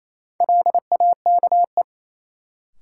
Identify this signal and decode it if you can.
Morse code, heard as a keyed tone, about 28 wpm, 700 Hz. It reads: LAKE
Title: Sentences from Top 1000 Words - Review Part 3 - New Word Drill - 28wpm